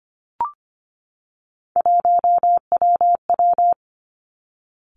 Morse code, heard as a keyed tone, about 25 wpm, 700 Hz. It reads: 1WW
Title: Three Challenging Letter-Number Combinations 25wpm